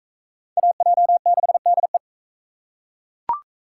Morse code, heard as a keyed tone, about 41 wpm, 700 Hz. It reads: AJ6DE